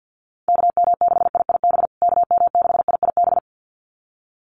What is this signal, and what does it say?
Morse code, heard as a keyed tone, about 50 wpm, 700 Hz. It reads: KM6IIB KM6IIB
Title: USA Call Signs Repeated 2x 50wpm